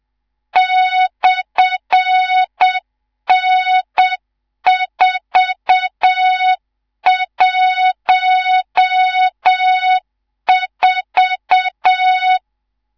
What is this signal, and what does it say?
Morse code, heard as a keyed tone, about 7 wpm, 3700 Hz, with noise. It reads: /N414
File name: Morze-HA2MN / hang / 19 GYAKORLAT